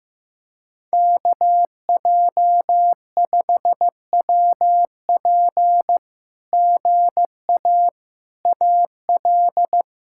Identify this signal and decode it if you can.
Morse code, heard as a keyed tone, about 14 wpm, 700 Hz. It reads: KJ5WP GA AL
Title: SOTA QSO Practice 15wpm